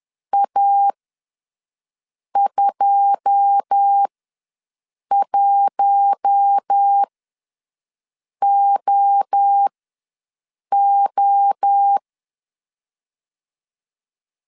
Morse code, heard as a keyed tone, about 10 wpm, spaced as for 5 wpm, 800 Hz. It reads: A21OO
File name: LEZIONI / 9 2867 ws 10wpm 4215